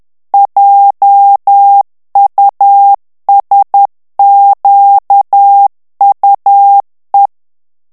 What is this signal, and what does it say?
Morse code, heard as a keyed tone, about 10 wpm, 800 Hz. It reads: JUSQUE
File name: FRANCAIS TEXTE CLAIR 2 VITESSE 10 MOTS MINUTES